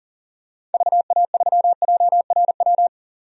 Morse code, heard as a keyed tone, about 40 wpm, 700 Hz. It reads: VA3JRW
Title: DXCC-100 Call Signs - 6 Characters - 40wpm